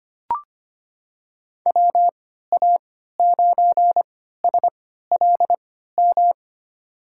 Morse code, heard as a keyed tone, about 25 wpm, spaced as for 14 wpm, 700 Hz. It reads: WA9SLM
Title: USA Call Signs - 6 Characters - 3x Character Spacing - 25wpm